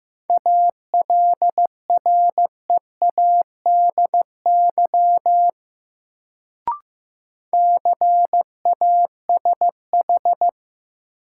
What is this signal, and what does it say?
Morse code, heard as a keyed tone, about 15 wpm, 700 Hz. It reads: ALREADY CASH
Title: Sentences from Top 700 Words - New Word Drill - 15wpm